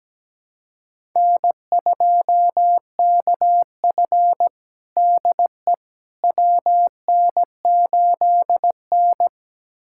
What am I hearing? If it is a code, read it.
Morse code, heard as a keyed tone, about 17 wpm, 700 Hz. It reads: N2KF DE WN8N
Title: USA Callsign DE Callsign - 4 Characters - 17wpm